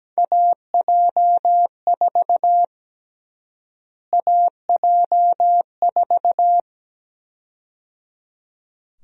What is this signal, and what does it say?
Morse code, heard as a keyed tone, about 17 wpm, 700 Hz. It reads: AJ4 AJ4
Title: USA Call Sign Prefixes - Repeated 2x - 17wpm